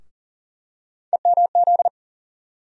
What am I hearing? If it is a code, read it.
Morse code, heard as a keyed tone, about 40 wpm, 700 Hz. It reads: EMZ
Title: USA Call Sign Postfixes - 3 Letters 40wpm